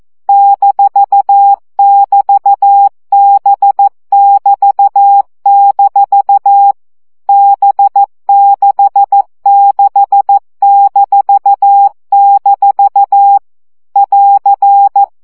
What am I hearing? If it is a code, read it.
Morse code, heard as a keyed tone, about 14 wpm, 800 Hz. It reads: -<BT>B<BT>- B66-- <AR>
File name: CW gyakorlas b 6 egyenlo minusz 12 baud